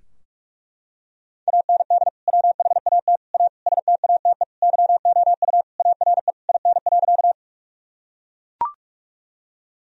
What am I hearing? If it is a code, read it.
Morse code, heard as a keyed tone, about 41 wpm, 700 Hz. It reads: AND WHAT A STATE YOU ARE IN.
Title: Sentences from Top 1000 Words - Review Part 2 - 45wpm